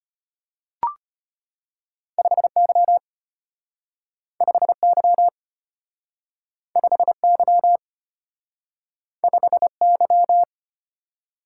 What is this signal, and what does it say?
Morse code, heard as a keyed tone, about 33 wpm, 700 Hz. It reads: 5Y 5Y 5Y 5Y